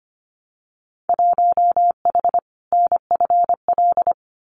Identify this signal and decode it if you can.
Morse code, heard as a keyed tone, about 25 wpm, 700 Hz. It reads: 1H NFL
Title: Winter Field Day Exchanges 25wpm